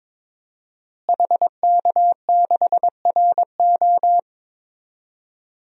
Morse code, heard as a keyed tone, about 22 wpm, 700 Hz. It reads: HK6RO
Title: DXCC-100 Call Signs - 5 Characters - Repeated 3x - 22wpm